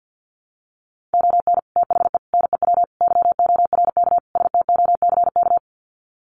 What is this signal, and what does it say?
Morse code, heard as a keyed tone, about 50 wpm, 700 Hz. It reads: ON THE NEW YORK STOCK
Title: Sets of 5 Words Encore 50wpm